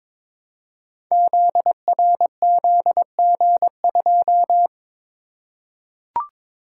Morse code, heard as a keyed tone, about 22 wpm, 700 Hz. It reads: ZRZG2
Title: Code Groups - 5 Letter:Numbers - 22wpm